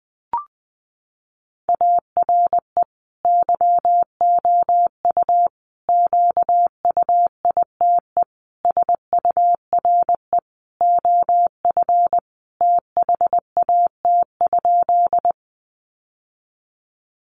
Morse code, heard as a keyed tone, about 20 wpm, 700 Hz. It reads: ARE YOU QUITE SURE OF THAT?
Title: Sentences from Top 700 Words 20wpm